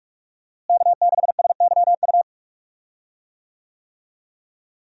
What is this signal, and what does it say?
Morse code, heard as a keyed tone, about 46 wpm, 700 Hz. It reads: K6SYU
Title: USA Call Signs - 5 Characters - Repeated 2x - 45wpm